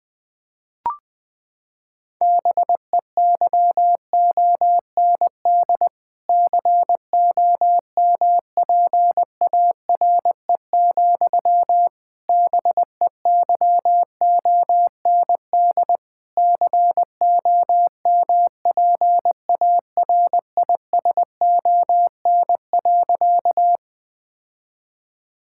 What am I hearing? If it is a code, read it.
Morse code, heard as a keyed tone, about 20 wpm, 700 Hz. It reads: BEYOND COMPARE, BEYOND COMPARISON.
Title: Sentences from Top 1600 Words 20wpm